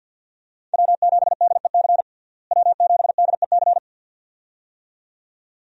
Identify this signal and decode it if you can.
Morse code, heard as a keyed tone, about 51 wpm, 700 Hz. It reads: W7DEC W7DEC
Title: USA Call Signs - 5 Characters - Repeated 2x - 50wpm